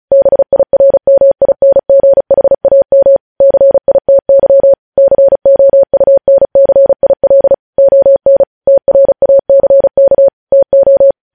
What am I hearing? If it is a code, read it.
Morse code, heard as a keyed tone, about 36 wpm, 550 Hz. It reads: BIRMINGHAM CITY COUNCIL ON TRACK TO
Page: news headlines in morse code at 35 wpm